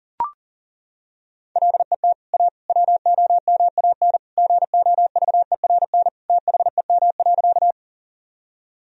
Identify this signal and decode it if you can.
Morse code, heard as a keyed tone, about 37 wpm, 700 Hz. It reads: LET A WOMAN GOVERN THEM.
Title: Sentences from Top 1000 Words - Review Part 2 - 40wpm